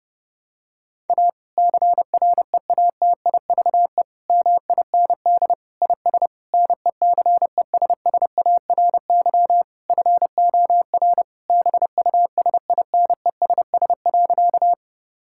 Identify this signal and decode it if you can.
Morse code, heard as a keyed tone, about 30 wpm, 700 Hz. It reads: A CREATIVE MIND IS NECESSARY FOR BUSINESS.
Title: Sentences from Top 1000 Words - Review Part 5 - 30wpm